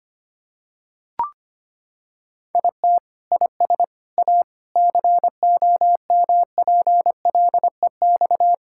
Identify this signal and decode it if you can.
Morse code, heard as a keyed tone, about 25 wpm, 700 Hz. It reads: IT IS A COMPLEX
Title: Sentences from Top 1000 Words - Review Part 6 - 25wpm